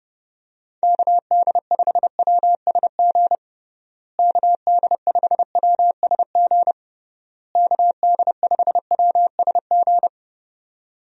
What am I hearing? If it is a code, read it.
Morse code, heard as a keyed tone, about 30 wpm, 700 Hz. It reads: KD5WSG KD5WSG KD5WSG
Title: USA Call Signs - 6 Characters - Repeated 3x - 30wpm